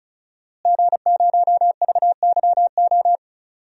Morse code, heard as a keyed tone, about 36 wpm, 700 Hz. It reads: G0VYO
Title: DXCC-100 Call Signs - 5 Characters - Repeated 3x - 35wpm